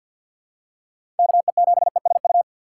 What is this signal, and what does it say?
Morse code, heard as a keyed tone, about 47 wpm, 700 Hz. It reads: KE7ESU
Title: USA Call Signs - 6 Characters - 4x Character Spacing - 50wpm